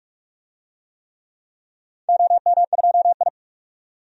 Morse code, heard as a keyed tone, about 46 wpm, 700 Hz. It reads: OM2I
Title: DXCC-100 Call Signs - 4 Characters - 45wpm